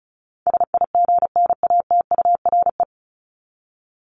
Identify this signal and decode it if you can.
Morse code, heard as a keyed tone, about 34 wpm, 700 Hz. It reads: SIGNATURE
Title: Top 1751-2000 Words 35wpm